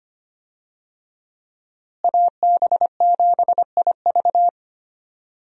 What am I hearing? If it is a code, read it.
Morse code, heard as a keyed tone, about 25 wpm, 700 Hz. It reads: AB7IV
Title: USA Call Signs - 5 Characters - Repeated 3x - 25wpm